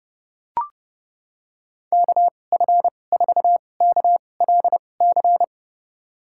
Morse code, heard as a keyed tone, about 30 wpm, spaced as for 22 wpm, 700 Hz. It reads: KF4KLC
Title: USA Call Signs - 6 Characters - 2x Character Spacing - 30wpm